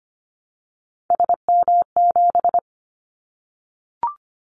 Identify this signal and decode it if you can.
Morse code, heard as a keyed tone, about 25 wpm, 700 Hz. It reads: SM7
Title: Callsign Prefixes for Top 25 DXCC - 25wpm